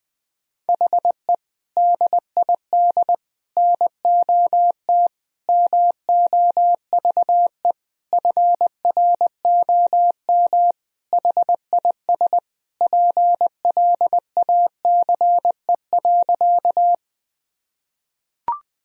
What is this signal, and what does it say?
Morse code, heard as a keyed tone, about 20 wpm, 700 Hz. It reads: HE DID NOT MOVE FROM HIS PLACE.
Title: Sentences from Top 1000 Words - Review Part 1 - 20wpm